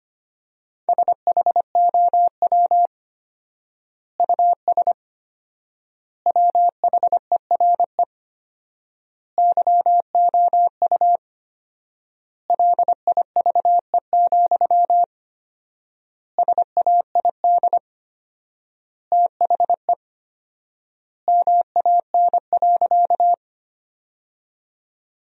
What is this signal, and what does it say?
Morse code, heard as a keyed tone, about 25 wpm, 700 Hz. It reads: SHOW US WHERE YOU LIVE, SAID THE MAN.
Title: 4x Word-Spacing - Sentences from Top 200 Words - 25wpm